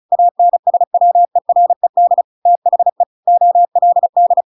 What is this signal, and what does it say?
Morse code, heard as a keyed tone, about 34 wpm, 700 Hz. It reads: ANSWERED THE OLD